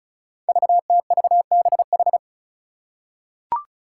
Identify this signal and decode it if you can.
Morse code, heard as a keyed tone, about 34 wpm, 700 Hz. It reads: VTVBH
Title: Five Challenging Letter-Number Combinations 35wpm